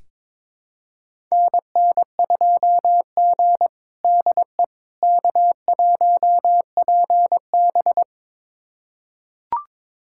Morse code, heard as a keyed tone, about 22 wpm, 700 Hz. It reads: NN2G DE K1PB
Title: USA Callsign DE Callsign - 4 Characters - 22wpm